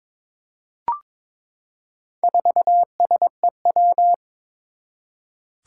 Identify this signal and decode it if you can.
Morse code, heard as a keyed tone, about 22 wpm, 700 Hz. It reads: 4SEW
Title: Four Challenging Letter-Number Combinations 22wpm